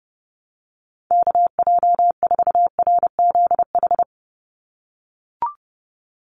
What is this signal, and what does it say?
Morse code, heard as a keyed tone, about 32 wpm, 700 Hz. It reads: KJ4RZH
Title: USA Call Signs 30wpm